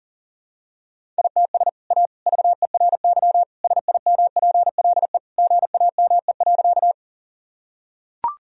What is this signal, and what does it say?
Morse code, heard as a keyed tone, about 40 wpm, 700 Hz. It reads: ITS A VERY SIMPLE GAME.
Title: Sentences from Top 1000 Words - Review Part 3 - 40wpm